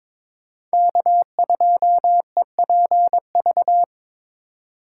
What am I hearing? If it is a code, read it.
Morse code, heard as a keyed tone, about 23 wpm, 700 Hz. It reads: K2EPV
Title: USA Call Signs - 5 Characters - 4x Character Spacing - 22wpm